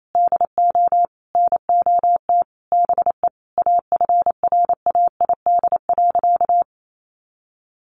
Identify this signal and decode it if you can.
Morse code, heard as a keyed tone, about 28 wpm, 700 Hz. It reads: DO NOT BE AFRAID.
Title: Sentences from Top 1500 Words 28wpm